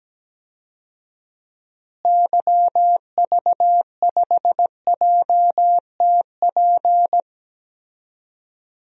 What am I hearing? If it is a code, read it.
Morse code, heard as a keyed tone, about 17 wpm, 700 Hz. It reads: YV5JTP
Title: DXCC-100 Call Signs - 6 Characters - Repeated 3x - 17wpm